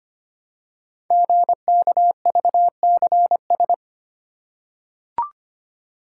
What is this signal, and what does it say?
Morse code, heard as a keyed tone, about 25 wpm, 700 Hz. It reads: GKVCS
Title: Code Groups - 5 Letters - 25wpm